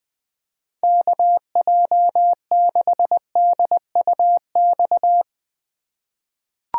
Morse code, heard as a keyed tone, about 21 wpm, 700 Hz. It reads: KJ6DUX
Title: USA Call Signs - 6 Characters - 3x Character Spacing - 20wpm